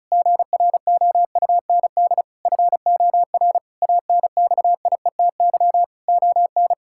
Morse code, heard as a keyed tone, about 36 wpm, 700 Hz. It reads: GROUND FOR ANXIETY ON
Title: Sentences from Top 1600 Words 35wpm